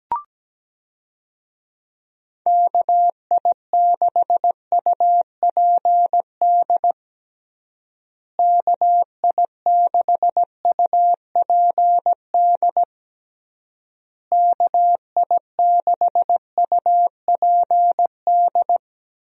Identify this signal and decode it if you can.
Morse code, heard as a keyed tone, about 17 wpm, 700 Hz. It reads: KI6UPD KI6UPD KI6UPD
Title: USA Call Signs - 6 Characters - Repeated 3x - 17wpm